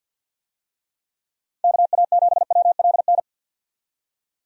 Morse code, heard as a keyed tone, about 50 wpm, 700 Hz. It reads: KA7WLN
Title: USA Call Signs - 6 Characters - 50wpm